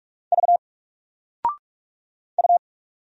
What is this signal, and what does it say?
Morse code, heard as a keyed tone, about 44 wpm, 700 Hz. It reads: V U
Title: Single Character (Letter:Number:Punctuation:Prosign) 45wpm